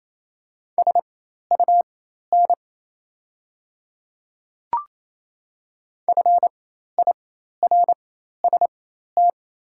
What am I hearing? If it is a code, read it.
Morse code, heard as a keyed tone, about 28 wpm, spaced as for 13 wpm, 700 Hz. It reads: SUN FIRST
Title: 4x Character Spacing - Top 300 Words - 28wpm